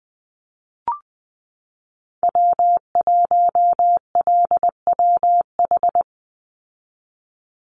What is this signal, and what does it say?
Morse code, heard as a keyed tone, about 19 wpm, 700 Hz. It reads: W1LWH
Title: Call Signs Repeated 2x - 20wpm